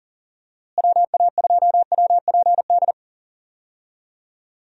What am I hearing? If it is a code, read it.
Morse code, heard as a keyed tone, about 40 wpm, 700 Hz. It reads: WA2WPD
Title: USA Call Signs - 6 Characters - Repeated 2x - 40wpm